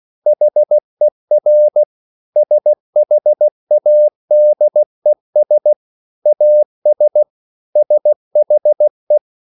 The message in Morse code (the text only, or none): HER SHADES AS SHE